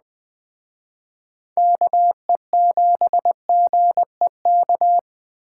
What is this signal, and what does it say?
Morse code, heard as a keyed tone, about 20 wpm, 700 Hz. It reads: KE7GEK